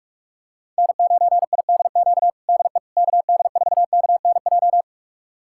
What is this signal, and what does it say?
Morse code, heard as a keyed tone, about 46 wpm, 700 Hz. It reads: N9IDQ DE KD4KNJ
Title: USA Callsign DE Callsign 45wpm